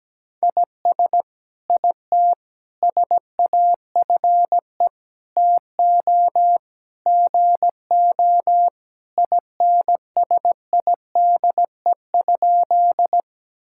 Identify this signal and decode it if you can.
Morse code, heard as a keyed tone, about 17 wpm, 700 Hz. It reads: IS IT SAFE TO GO INSIDE?
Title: Sentences from Top 1000 Words - Review Part 6 - 17wpm